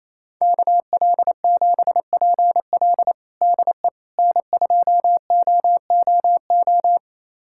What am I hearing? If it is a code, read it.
Morse code, heard as a keyed tone, about 28 wpm, 700 Hz. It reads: KL7PL DE N2OOO
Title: USA Callsign DE Callsign - 5 Characters - 28wpm